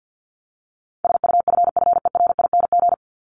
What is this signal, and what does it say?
Morse code, heard as a keyed tone, about 50 wpm, 700 Hz. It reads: SUFFERING
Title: ING Suffix - Top 100 Words with Base Words 50wpm